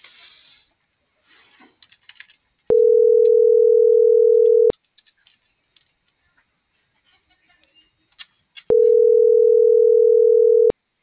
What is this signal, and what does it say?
Ringback tone (North America), 440 + 480 Hz.